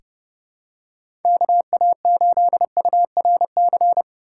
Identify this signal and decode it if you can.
Morse code, heard as a keyed tone, about 30 wpm, 700 Hz. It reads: KA8URC
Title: USA Call Signs - 6 Characters - 2x Character Spacing - 30wpm